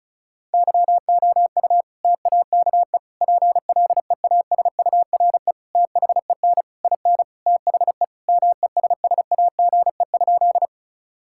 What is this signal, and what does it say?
Morse code, heard as a keyed tone, about 34 wpm, 700 Hz. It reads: YOU TAKE PLEASURE THEN IN THE MESSAGE?
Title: Sentences from Top 1600 Words 35wpm